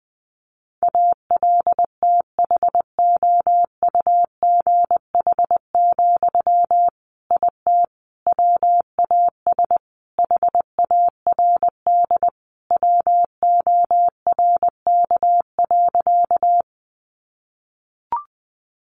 Morse code, heard as a keyed tone, about 20 wpm, 700 Hz. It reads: ALTHOUGH, IT WAS HARD WORK.
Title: Sentences from Top 600 Words 20wpm